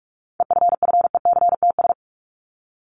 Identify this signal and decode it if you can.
Morse code, heard as a keyed tone, about 44 wpm, 700 Hz. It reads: EFFECTS
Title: ES:S Suffix - Top 100 Words with Base Words 45wpm